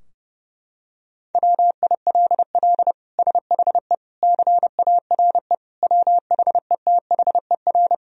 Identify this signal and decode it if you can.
Morse code, heard as a keyed tone, about 30 wpm, 700 Hz. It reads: WILL SHE CARE WHETHER
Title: Sentences from Top 1000 Words - Review Part 3 - 30wpm